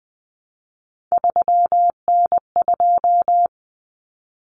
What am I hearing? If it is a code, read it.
Morse code, heard as a keyed tone, about 20 wpm, 700 Hz. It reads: 3N2
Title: Common Cut Numbers - 3-Digit 20wpm